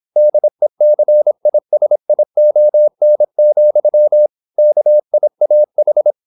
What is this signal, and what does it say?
Morse code, heard as a keyed tone, about 26 wpm, 600 Hz. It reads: DECISION, KIAH
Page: Never 26wpm | CW med Gnister